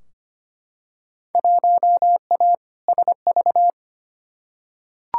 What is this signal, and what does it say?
Morse code, heard as a keyed tone, about 25 wpm, 700 Hz. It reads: 1A SV
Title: Field Day Exchanges Encore - 25wpm